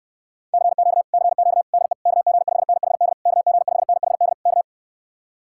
Morse code, heard as a keyed tone, about 69 wpm, 700 Hz. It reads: CQ CQ DE KG5NHD KG5NHD K